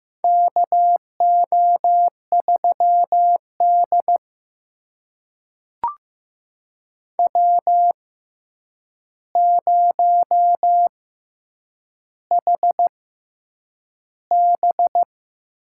Morse code, heard as a keyed tone, about 15 wpm, spaced as for 8 wpm, 700 Hz. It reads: KO3D W0HB
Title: USA Call Signs - 4 Characters - 6x Character Spacing - 15wpm